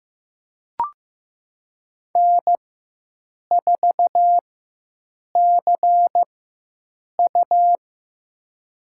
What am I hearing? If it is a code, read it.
Morse code, heard as a keyed tone, about 15 wpm, spaced as for 7 wpm, 700 Hz. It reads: N4CU
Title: USA Call Signs - 4 Characters - 4x Character Spacing - 15wpm